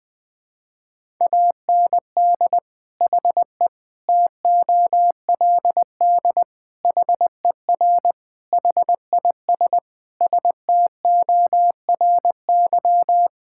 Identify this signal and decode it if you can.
Morse code, heard as a keyed tone, about 21 wpm, 700 Hz. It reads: AND HE TOLD HER HIS STORY
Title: Sentences from Top 1000 Words - Review Part 5 - 20wpm